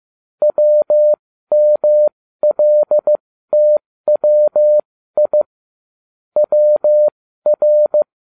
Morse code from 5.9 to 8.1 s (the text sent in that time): WR